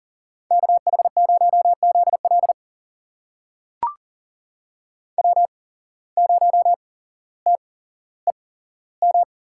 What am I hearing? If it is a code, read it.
Morse code, heard as a keyed tone, about 40 wpm, spaced as for 17 wpm, 700 Hz. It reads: KH0ZL W0TEM
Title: USA Call Signs - 5 Characters - 8x Character Spacing - 40wpm